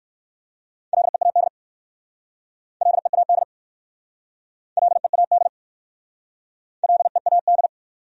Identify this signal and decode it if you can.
Morse code, heard as a keyed tone, about 58 wpm, 700 Hz. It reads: LEAD LEAD LEAD LEAD